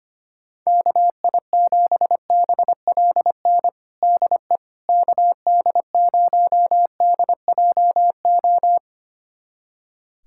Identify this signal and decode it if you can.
Morse code, heard as a keyed tone, about 25 wpm, 700 Hz. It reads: KI7BLN DE KD0DJO
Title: USA Callsign DE Callsign 25wpm